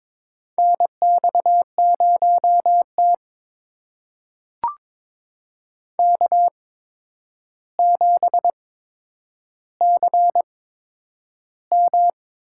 Morse code, heard as a keyed tone, about 22 wpm, spaced as for 9 wpm, 700 Hz. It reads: NX0T K7CM
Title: USA Call Signs - 4 Characters - 8x Character Spacing - 22wpm